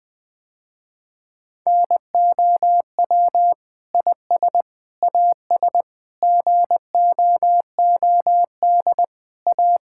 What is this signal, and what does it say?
Morse code, heard as a keyed tone, about 20 wpm, 700 Hz. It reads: NOW IS AS GOOD A
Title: Sentences with 8-Word Phrases - Spoken Beginning - 20wpm